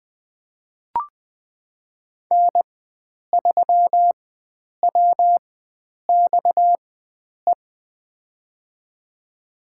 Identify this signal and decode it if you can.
Morse code, heard as a keyed tone, about 20 wpm, spaced as for 9 wpm, 700 Hz. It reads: N3WXE
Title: USA Call Signs - 5 Characters - 4x Character Spacing - 20wpm